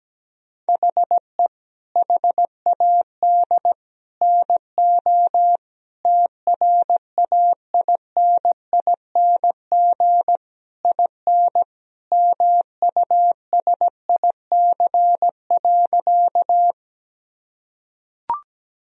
Morse code, heard as a keyed tone, about 17 wpm, 700 Hz. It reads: HE HAD NO TRAINING IN MUSIC.
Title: Sentences from Top 1000 Words - Review Part 4 - 17wpm